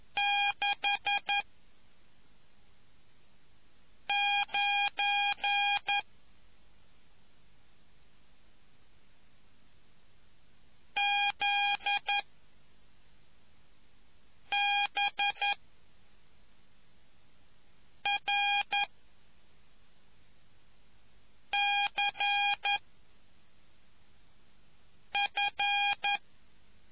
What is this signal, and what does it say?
Morse code, heard as a keyed tone, about 11 wpm, 2450 Hz. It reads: 6 9 Z B R C F